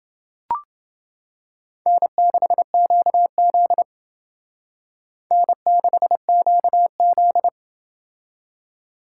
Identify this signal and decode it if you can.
Morse code, heard as a keyed tone, about 28 wpm, 700 Hz. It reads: N6QZ N6QZ